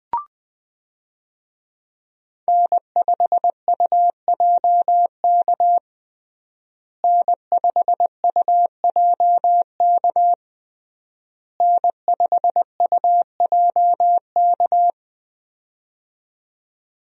Morse code, heard as a keyed tone, about 20 wpm, 700 Hz. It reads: N5UJK N5UJK N5UJK
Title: USA Call Signs - 5 Characters - Repeated 3x - 20wpm